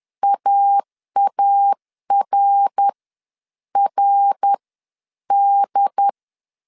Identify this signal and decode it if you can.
Morse code, heard as a keyed tone, about 10 wpm, spaced as for 8 wpm, 800 Hz. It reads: AARRD
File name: 10wpm